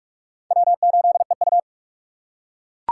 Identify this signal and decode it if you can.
Morse code, heard as a keyed tone, about 39 wpm, 700 Hz. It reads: W8EU